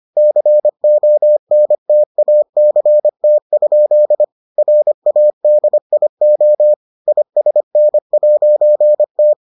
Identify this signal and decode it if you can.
Morse code, heard as a keyed tone, about 25 wpm, 600 Hz. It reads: CONTACT? RADIO ISN'T